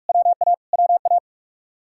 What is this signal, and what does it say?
Morse code, heard as a keyed tone, about 56 wpm, 700 Hz. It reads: WA WA
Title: Speed-Racing Common QSO Phrases - 45wpm